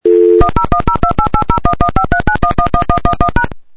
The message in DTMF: 1010288011539111111#